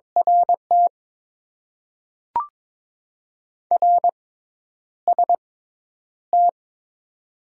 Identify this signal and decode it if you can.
Morse code, heard as a keyed tone, about 22 wpm, spaced as for 10 wpm, 700 Hz. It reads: RT RST
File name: Top 90 QSO Elements - 6x Character Spacing - 22wpm